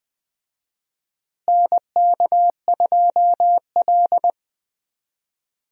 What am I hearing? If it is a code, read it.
Morse code, heard as a keyed tone, about 20 wpm, 700 Hz. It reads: NK2L
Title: USA Call Signs - 4 Characters - Repeated 3x - 20wpm